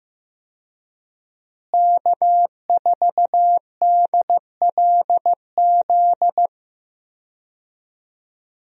Morse code, heard as a keyed tone, about 15 wpm, 700 Hz. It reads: K4DLZ
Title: USA Call Signs - 5 Characters - Repeated 3x - 15wpm